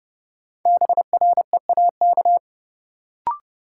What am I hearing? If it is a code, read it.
Morse code, heard as a keyed tone, about 32 wpm, 700 Hz. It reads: BREAK
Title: Sentences from Top 900 Words - New Word Drill - 30wpm